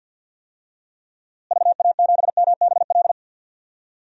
Morse code, heard as a keyed tone, about 48 wpm, 700 Hz. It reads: VA7MBL
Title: DXCC-100 Call Signs - 6 Characters - Repeated 2x - 50wpm